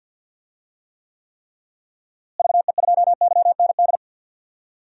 Morse code, heard as a keyed tone, about 52 wpm, 700 Hz. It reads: VE2YND
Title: DXCC-100 Call Signs - 6 Characters - Repeated 3x - 50wpm